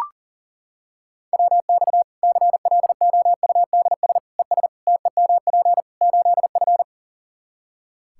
Morse code, heard as a keyed tone, about 40 wpm, 700 Hz. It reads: WX CLOUDS ES TEMP 8F
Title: Practice QSO - Exchange 2 - Info - 40wpm